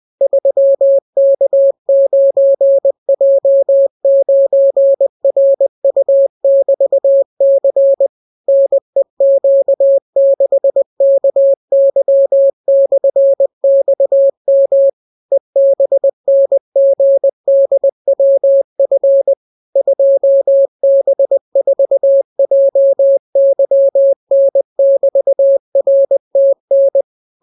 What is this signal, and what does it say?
Morse code, heard as a keyed tone, about 20 wpm, 550 Hz. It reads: 3K9J9RU<BT>C NEQ6KY/XM EBNGDWF 2B4JYN<BT>RTN